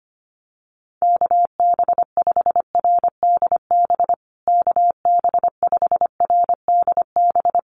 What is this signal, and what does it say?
Morse code, heard as a keyed tone, about 25 wpm, 700 Hz. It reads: KB5RDB KB5RDB
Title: USA Call Signs Repeated 3x 25wpm